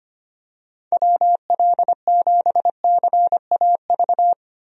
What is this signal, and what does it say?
Morse code, heard as a keyed tone, about 25 wpm, 700 Hz. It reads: WL7CAV
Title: USA Call Signs - 6 Characters - Repeated 3x - 25wpm